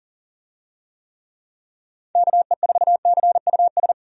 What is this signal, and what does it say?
Morse code, heard as a keyed tone, about 40 wpm, 700 Hz. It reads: KE4CUS
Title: USA Call Signs - 6 Characters - 40wpm